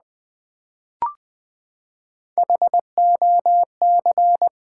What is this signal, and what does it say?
Morse code, heard as a keyed tone, about 20 wpm, 700 Hz. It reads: HOC
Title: Build A Word - 3 Letters At A Time - 6-Letter Words - 20wpm